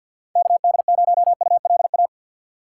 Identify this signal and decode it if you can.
Morse code, heard as a keyed tone, about 49 wpm, 700 Hz. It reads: KD0ULA
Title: USA Call Signs - 6 Characters - 6x Character Spacing - 50wpm